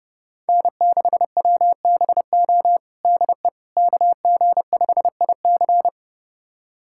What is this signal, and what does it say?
Morse code, heard as a keyed tone, about 30 wpm, 700 Hz. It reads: N6WBO DE KG5IC
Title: USA Callsign DE Callsign - 5 Characters - 30wpm